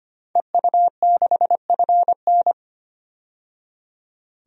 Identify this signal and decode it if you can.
Morse code, heard as a keyed tone, about 25 wpm, 700 Hz. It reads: EU6FN
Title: DXCC-100 Call Signs - 5 Characters - 25wpm